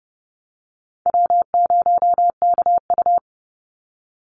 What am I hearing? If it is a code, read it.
Morse code, heard as a keyed tone, about 30 wpm, 700 Hz. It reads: W0KU
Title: Easy Call Signs Repeated 2x - 30wpm